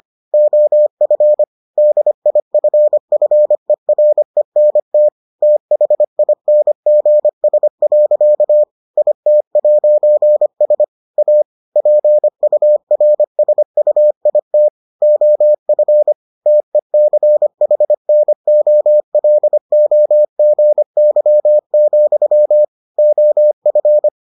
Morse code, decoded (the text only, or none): OF DIFFERENT THINGS. IT'S A PURSUIT OF TECHNOLOGY, OF